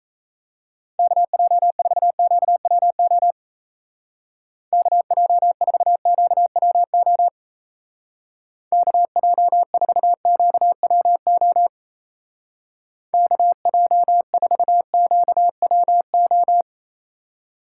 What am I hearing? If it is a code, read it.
Morse code, heard as a keyed tone, about 34 wpm, 700 Hz. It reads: KJ4QWO KJ4QWO KJ4QWO KJ4QWO
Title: Speed Racing - USA Call Signs 28wpm